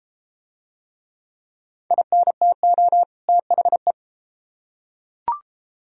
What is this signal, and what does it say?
Morse code, heard as a keyed tone, about 33 wpm, 700 Hz. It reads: INTO THE